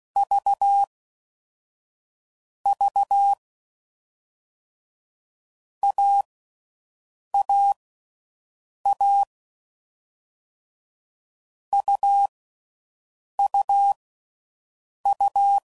Morse code, heard as a keyed tone, about 16 wpm, spaced as for 6 wpm, 800 Hz. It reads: V V AAA UUU
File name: Combinations / Difficult Combos / AUV ShortRun